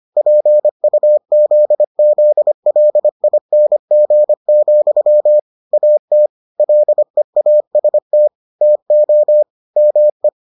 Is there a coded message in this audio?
Yes: Morse code from 0.2 to 10.3 s, heard as a keyed tone, about 25 wpm, 600 Hz. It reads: PUZZLING, AT LEAST TO ME